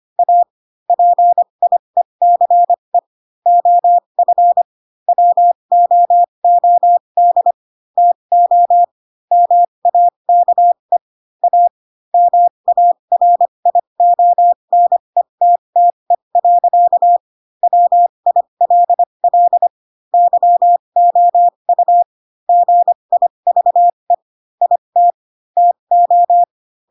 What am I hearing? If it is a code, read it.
Morse code, heard as a keyed tone, about 25 wpm, spaced as for 22 wpm, 700 Hz. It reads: A PIECE OF WOOD TO MAKE A MARIONETTE. WILL YOU GIVE IT TO
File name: Pinocchio / Pinocchio ch2 18 25 ik1hge